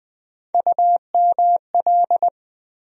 Morse code, heard as a keyed tone, about 20 wpm, 700 Hz. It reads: UML
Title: USA Call Sign Postfixes - 3 Letters 20wpm